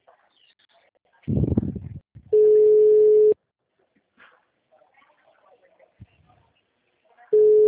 Ringback tone (Europe, 425 Hz).